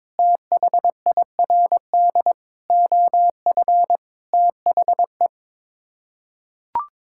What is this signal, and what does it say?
Morse code, heard as a keyed tone, about 23 wpm, 700 Hz. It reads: THIRD OF THE
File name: practice / 22wpm / Sets of 6 Words - 22wpm